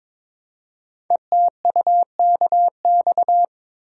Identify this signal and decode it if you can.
Morse code, heard as a keyed tone, about 22 wpm, 700 Hz. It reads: ETUKX
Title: Code Groups - 5 Letters - 22wpm